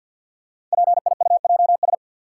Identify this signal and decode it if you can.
Morse code, heard as a keyed tone, about 48 wpm, 700 Hz. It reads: PIUJS
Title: Code Groups - 5 Letters - 50wpm